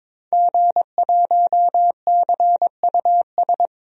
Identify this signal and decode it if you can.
Morse code, heard as a keyed tone, about 22 wpm, 700 Hz. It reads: G1CUS
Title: DXCC-100 Call Signs - 5 Characters - 22wpm